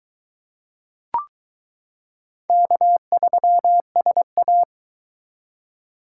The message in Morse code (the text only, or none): K3SA